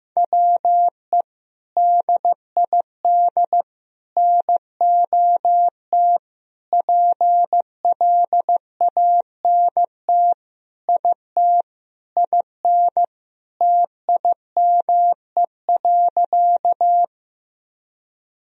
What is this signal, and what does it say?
Morse code, heard as a keyed tone, about 15 wpm, 700 Hz. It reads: WE DID NOT PLANT IT IN TIME.
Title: Sentences from Top 1000 Words - Review Part 1 - 15wpm